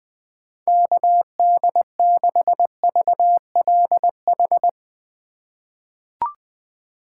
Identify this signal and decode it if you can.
Morse code, heard as a keyed tone, about 20 wpm, 700 Hz. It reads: KD6VLH
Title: USA Call Signs - 6 Characters - 3x Character Spacing - 20wpm